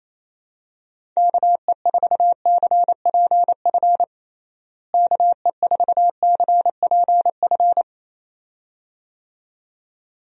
Morse code, heard as a keyed tone, about 28 wpm, 700 Hz. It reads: KE4CPF KE4CPF
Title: USA Call Signs - 6 Characters - Repeated 2x - 28wpm